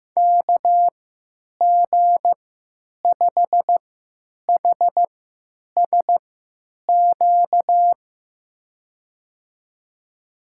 Morse code, heard as a keyed tone, about 15 wpm, spaced as for 9 wpm, 700 Hz. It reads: KG5HSQ
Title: USA Call Signs - 6 Characters - 3x Character Spacing - 15wpm